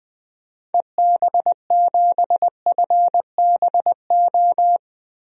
Morse code, heard as a keyed tone, about 20 wpm, 690 Hz. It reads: EB7FBO
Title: DXCC-100 Call Signs - 6 Characters - Repeated 2x - 20wpm